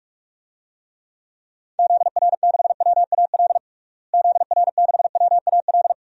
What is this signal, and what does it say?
Morse code, heard as a keyed tone, about 46 wpm, 700 Hz. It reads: ZR6WAL ZR6WAL
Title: DXCC-100 Call Signs - 6 Characters - Repeated 3x - 45wpm